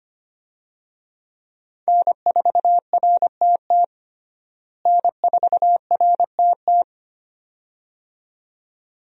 Morse code, heard as a keyed tone, about 25 wpm, 700 Hz. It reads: N4RTT N4RTT
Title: USA Call Signs - 5 Characters - Repeated 2x - 25wpm